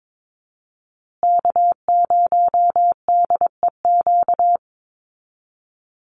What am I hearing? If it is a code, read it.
Morse code, heard as a keyed tone, about 22 wpm, 700 Hz. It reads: K0DEQ
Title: Call Signs Repeated 2x - 22wpm